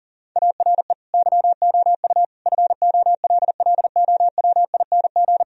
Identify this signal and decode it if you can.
Morse code, heard as a keyed tone, about 37 wpm, 700 Hz. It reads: ARE YOU FOLLOWING
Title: Sentences from Top 1000 Words - Review Part 4 - 40wpm